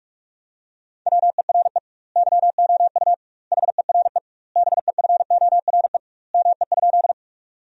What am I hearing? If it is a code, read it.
Morse code, heard as a keyed tone, about 46 wpm, 700 Hz. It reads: WERE YOU HERE BEFORE ME?
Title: 2x Word-Spacing - Sentences from Top 200 Words - 45wpm